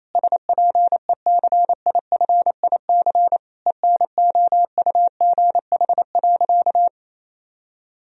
Morse code, heard as a keyed tone, about 27 wpm, 700 Hz. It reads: SPECIFIC ENOUGH.
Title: Sentences from Top 1000 Words - Review Part 3 - 28wpm